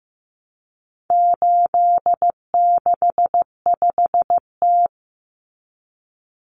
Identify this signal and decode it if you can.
Morse code, heard as a keyed tone, about 15 wpm, 700 Hz. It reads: 865T
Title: Common Cut Numbers - 4-Digit 15wpm